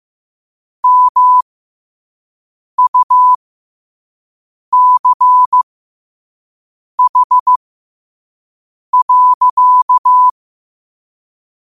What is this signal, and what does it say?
Morse code, heard as a keyed tone, about 15 wpm, spaced as for 5 wpm, 1000 Hz. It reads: MUCH.
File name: qotdfiles / 5wpm